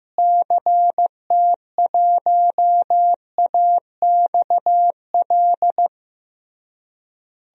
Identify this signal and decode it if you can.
Morse code, heard as a keyed tone, about 15 wpm, 700 Hz. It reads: CT1AXL
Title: DXCC-100 Call Signs - 6 Characters - Repeated 3x - 15wpm